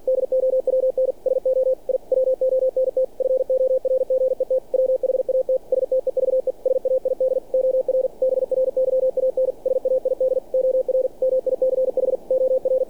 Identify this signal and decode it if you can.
Morse code, heard as a keyed tone, about 48 wpm, 520 Hz. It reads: DOWN SO I WONT FORGET WHAT STEVE SAID OR BRYAN SAID OR MICH OR
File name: archived / CRSnet